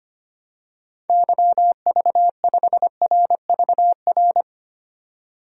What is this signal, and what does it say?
Morse code, heard as a keyed tone, about 25 wpm, 700 Hz. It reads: YV5RVR
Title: DXCC-100 Call Signs - 6 Characters - 25wpm